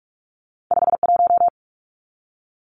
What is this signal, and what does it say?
Morse code, heard as a keyed tone, about 46 wpm, 700 Hz. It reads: 51
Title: Numbers - 2-Digit 45wpm